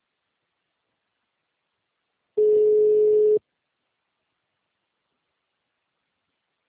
Ringback tone (Europe, 425 Hz).